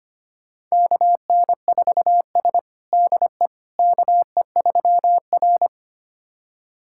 Morse code, heard as a keyed tone, about 25 wpm, 700 Hz. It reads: KN4S DE KE3R
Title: USA Callsign DE Callsign - 4 Characters - 25wpm